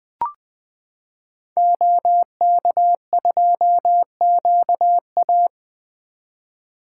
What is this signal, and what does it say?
Morse code, heard as a keyed tone, about 20 wpm, 700 Hz. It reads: OK2QA
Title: 2023 SOTA Activator Call Signs - 20wpm